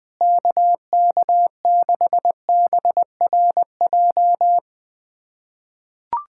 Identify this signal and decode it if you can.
Morse code, heard as a keyed tone, about 21 wpm, 700 Hz. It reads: KK6BRJ
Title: USA Callsign DE Callsign - 6 Characters - 20wpm